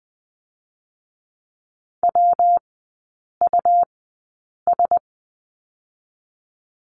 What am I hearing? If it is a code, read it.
Morse code, heard as a keyed tone, about 20 wpm, 700 Hz. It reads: W U S
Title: 121 Lesson 20e - Number 3 - Three Characters 20wpm